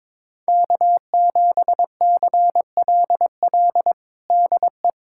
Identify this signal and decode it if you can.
Morse code, heard as a keyed tone, about 22 wpm, 700 Hz. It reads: K7CLL DE
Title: USA Callsign DE Callsign - 5 Characters - 22wpm